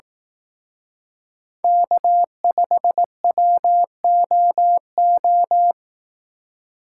Morse code, heard as a keyed tone, about 18 wpm, 700 Hz. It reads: K5WOO
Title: Speed Racing - USA 1x3 Call Signs 15wpm